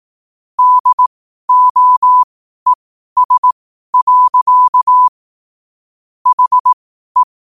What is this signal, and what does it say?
Morse code, heard as a keyed tone, about 18 wpm, spaced as for 12 wpm, 1000 Hz. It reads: DOES. HE
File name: qotdfiles / 12wpm